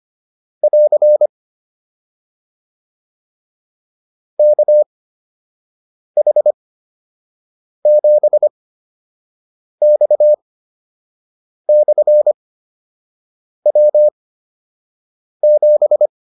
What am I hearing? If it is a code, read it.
Morse code, heard as a keyed tone, about 25 wpm, spaced as for 6 wpm, 600 Hz. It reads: <AR> KH7X/W7